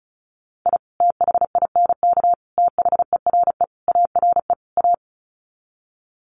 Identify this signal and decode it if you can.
Morse code, heard as a keyed tone, about 34 wpm, 700 Hz. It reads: I THINK THERE ARE A
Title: Sets of 5 Words Encore2 35wpm